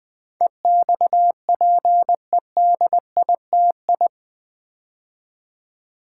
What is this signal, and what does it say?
Morse code, heard as a keyed tone, about 20 wpm, 700 Hz. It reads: EXPEDITI
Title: Build A Word - 4 Letters At A Time - 10-Letter Words - 20wpm